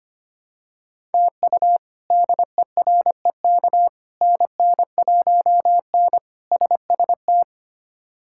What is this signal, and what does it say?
Morse code, heard as a keyed tone, about 25 wpm, 700 Hz. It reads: TU DEREK NN1N SST
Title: K1USN SST QSO Practice - 25wpm